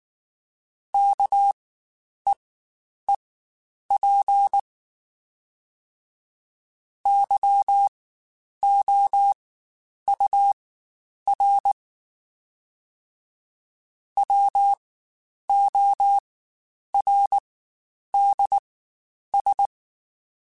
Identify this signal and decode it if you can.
Morse code, heard as a keyed tone, about 19 wpm, spaced as for 9 wpm, 800 Hz. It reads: KEEP YOUR WORDS
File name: Spaced / Chuckles Spaced-mp3s / SoftWords sp @18